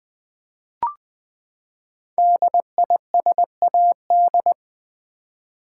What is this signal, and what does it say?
Morse code, heard as a keyed tone, about 20 wpm, 700 Hz. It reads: DISAD
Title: Build A Word - 5 Letters At A Time - 12-Letter Words - 20wpm